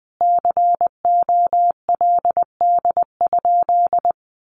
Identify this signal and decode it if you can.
Morse code, heard as a keyed tone, about 20 wpm, 700 Hz. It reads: COLD?
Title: A Sound Of Thunder 20wpm